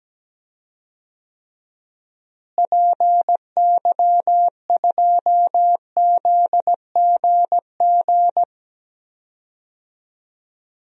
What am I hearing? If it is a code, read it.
Morse code, heard as a keyed tone, about 17 wpm, 700 Hz. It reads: PY2ZGG
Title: DXCC-100 Call Signs - 6 Characters - Repeated 2x - 17wpm